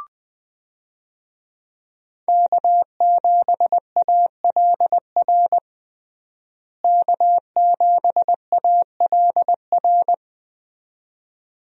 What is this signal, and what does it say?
Morse code, heard as a keyed tone, about 20 wpm, 700 Hz. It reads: K7ALR K7ALR
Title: USA Call Signs - 5 Characters - Repeated 2x - 20wpm